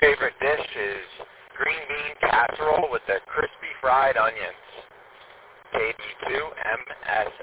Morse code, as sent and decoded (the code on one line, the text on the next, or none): none